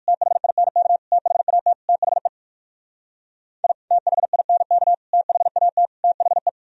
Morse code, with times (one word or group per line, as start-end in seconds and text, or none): 0.1-1.0 THINK
1.1-1.7 THAT
1.9-2.3 THE
3.6-3.7 I
3.9-4.9 THINK
5.1-5.9 THAT
6.0-6.5 THE